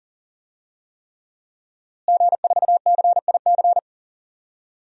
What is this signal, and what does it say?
Morse code, heard as a keyed tone, about 39 wpm, 700 Hz. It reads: G4CIC
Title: DXCC-100 Call Signs - 5 Characters - Repeated 3x - 40wpm